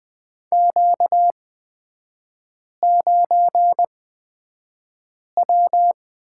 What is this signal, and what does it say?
Morse code, heard as a keyed tone, about 20 wpm, spaced as for 5 wpm, 700 Hz. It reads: Q9W
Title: 236 Lesson 38e - Letter q - Three Characters 20wpm